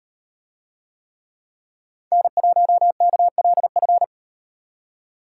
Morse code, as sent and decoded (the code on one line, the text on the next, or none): -. .---- -.- .-.. ..-.
N1KLF